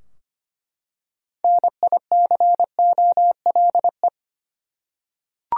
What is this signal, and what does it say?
Morse code, heard as a keyed tone, about 25 wpm, 700 Hz. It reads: NICOLE
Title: USA Common Names - 25wpm